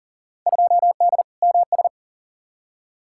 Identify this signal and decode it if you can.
Morse code, heard as a keyed tone, about 37 wpm, 700 Hz. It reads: 2D MS
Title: Field Day Exchanges Encore - 40wpm